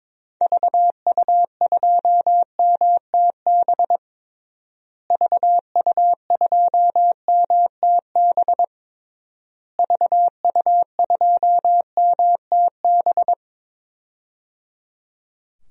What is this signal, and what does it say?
Morse code, heard as a keyed tone, about 22 wpm, 700 Hz. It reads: VU2MTB VU2MTB VU2MTB
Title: DXCC-100 Call Signs - 6 Characters - Repeated 3x - 22wpm